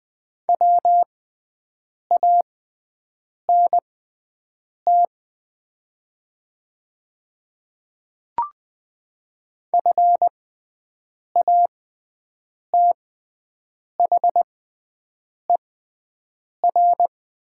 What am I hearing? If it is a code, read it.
Morse code, heard as a keyed tone, about 20 wpm, spaced as for 7 wpm, 700 Hz. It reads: WANT FATHER
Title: 6x Character Spacing - Top 200 Words - 20wpm